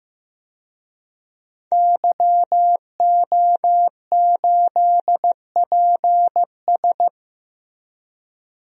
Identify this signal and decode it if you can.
Morse code, heard as a keyed tone, about 15 wpm, 700 Hz. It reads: YO8PS
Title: DXCC-100 Call Signs - 5 Characters - 15wpm